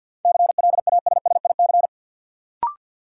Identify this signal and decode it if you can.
Morse code, heard as a keyed tone, about 52 wpm, 700 Hz. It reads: CLASSIC
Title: Sentences from Top 1000 Words - New Word Drill - 50wpm